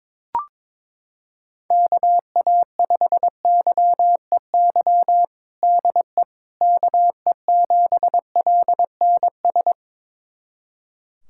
Morse code, heard as a keyed tone, about 22 wpm, 700 Hz. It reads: KA5YEY DE KE7LNS
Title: USA Callsign DE Callsign 22wpm